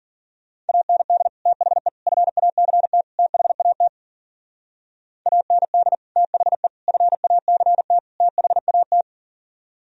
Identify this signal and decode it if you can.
Morse code, heard as a keyed tone, about 51 wpm, 700 Hz. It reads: AND THE FACT THAT AND THE FACT THAT